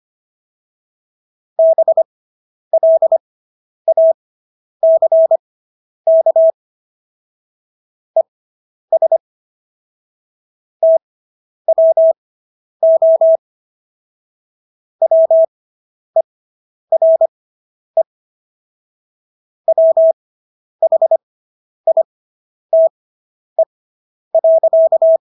Morse code, heard as a keyed tone, about 25 wpm, spaced as for 10 wpm, 650 Hz. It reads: BLACK ES TWO WERE WHITE.